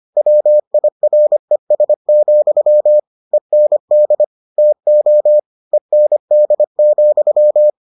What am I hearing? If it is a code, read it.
Morse code, heard as a keyed tone, about 25 wpm, 600 Hz. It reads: WIRES, END TO END,